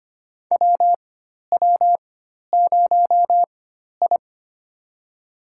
Morse code, heard as a keyed tone, about 25 wpm, spaced as for 12 wpm, 690 Hz. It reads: WW0I
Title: USA Call Signs - 4 Characters - 4x Character Spacing - 25wpm